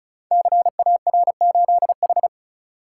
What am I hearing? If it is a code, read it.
Morse code, heard as a keyed tone, about 36 wpm, 700 Hz. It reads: CAR8H
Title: Code Groups - 5 Letter:Numbers - 35wpm